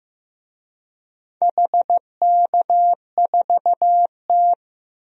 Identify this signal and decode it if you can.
Morse code, heard as a keyed tone, about 15 wpm, 700 Hz. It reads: HK4T
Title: DXCC-100 Call Signs - 4 Characters - Repeated 2x - 15wpm